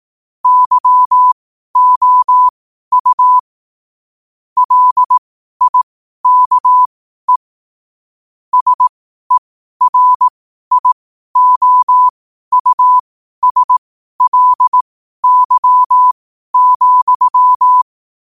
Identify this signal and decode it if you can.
Morse code, heard as a keyed tone, about 18 wpm, spaced as for 12 wpm, 1000 Hz. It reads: YOU LIKE SERIOUSLY,
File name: qotdfiles / 12wpm